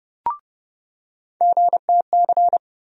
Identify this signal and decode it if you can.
Morse code, heard as a keyed tone, about 30 wpm, 700 Hz. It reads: GTC
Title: USA Call Sign Postfixes - 3 Letters 30wpm